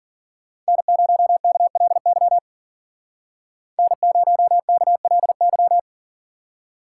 Morse code, heard as a keyed tone, about 52 wpm, 700 Hz. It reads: N0KLY N0KLY